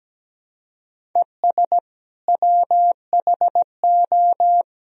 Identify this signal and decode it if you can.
Morse code, heard as a keyed tone, about 17 wpm, 700 Hz. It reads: ES WHO
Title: Sentences from Top 100 Words and Abbreviations 17wpm